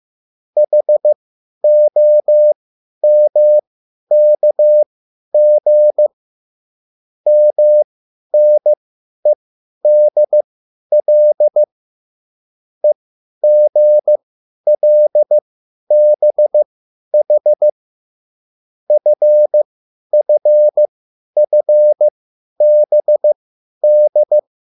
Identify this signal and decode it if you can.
Morse code, heard as a keyed tone, about 15 wpm, spaced as for 10 wpm, 600 Hz. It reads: HOMKG MNEDL EGLBH FFFBD
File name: QSL / CW lesson